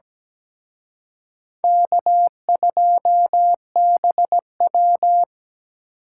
Morse code, heard as a keyed tone, about 20 wpm, 700 Hz. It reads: K2BW
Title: Speed Racing - USA Call Signs 17wpm